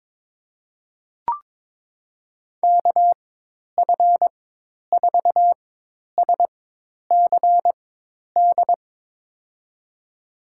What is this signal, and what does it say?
Morse code, heard as a keyed tone, about 21 wpm, spaced as for 10 wpm, 700 Hz. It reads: KF4SCD
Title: USA Call Signs - 4x Character Spacing - 22wpm